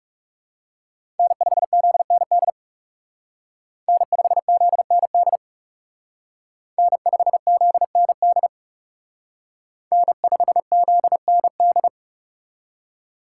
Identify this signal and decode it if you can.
Morse code, heard as a keyed tone, about 38 wpm, 700 Hz. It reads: N5ZND N5ZND N5ZND N5ZND